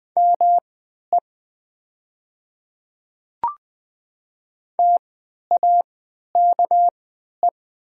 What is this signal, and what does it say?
Morse code, heard as a keyed tone, about 20 wpm, spaced as for 11 wpm, 700 Hz. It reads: ME TAKE
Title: 3x Character Spacing - Top 500 Words - 20wpm